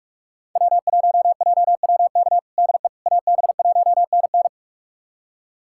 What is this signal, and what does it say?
Morse code, heard as a keyed tone, about 41 wpm, 700 Hz. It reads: W1JWK DE AB1NN
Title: USA Callsign DE Callsign - 5 Characters - 45wpm